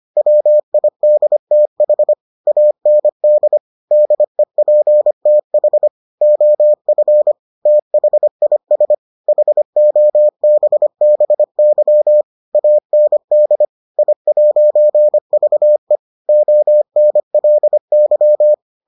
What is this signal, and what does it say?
Morse code, heard as a keyed tone, about 25 wpm, 600 Hz. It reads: WIDTH AND DEPTH OF THIS HOBBY AND I'VE ONLY